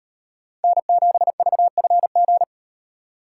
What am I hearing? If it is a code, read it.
Morse code, heard as a keyed tone, about 37 wpm, 700 Hz. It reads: N7VFG